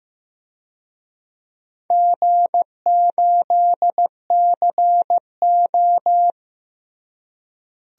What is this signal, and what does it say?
Morse code, heard as a keyed tone, about 15 wpm, 700 Hz. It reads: G8CO
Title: DXCC-100 Call Signs - 4 Characters - Repeated 2x - 15wpm